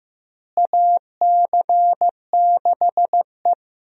Morse code, heard as a keyed tone, about 15 wpm, 700 Hz. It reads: AC6E
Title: USA Call Signs - 4 Characters - Repeated 2x - 15wpm